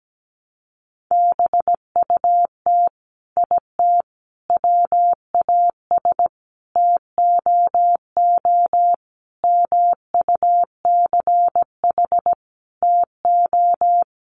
Morse code, heard as a keyed tone, about 17 wpm, 700 Hz. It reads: BUT IT WAS TOO MUCH TO
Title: Sentences from Top 1100 Words 17wpm